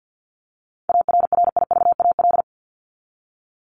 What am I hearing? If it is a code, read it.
Morse code, heard as a keyed tone, about 50 wpm, 700 Hz. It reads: ARRIVAL
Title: Top 1751-2000 Words 50wpm